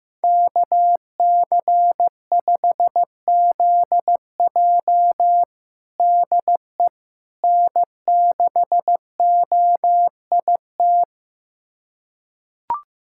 Morse code, heard as a keyed tone, about 15 wpm, 700 Hz. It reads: KC5ZJ DE N6OIT
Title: USA Callsign DE Callsign - 5 Characters - 15wpm